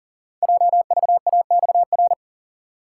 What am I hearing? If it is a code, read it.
Morse code, heard as a keyed tone, about 38 wpm, 700 Hz. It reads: JVAXR
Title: Code Groups - 5 Letter:Numbers - 40wpm